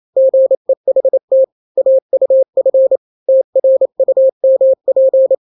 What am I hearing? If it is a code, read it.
Morse code, heard as a keyed tone, about 28 wpm, 530 Hz. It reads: GEHT AUF TRUMP
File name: daily / 27wpm